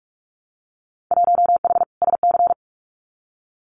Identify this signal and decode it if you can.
Morse code, heard as a keyed tone, about 46 wpm, 700 Hz. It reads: 1H SC
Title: Winter Field Day Exchanges 45wpm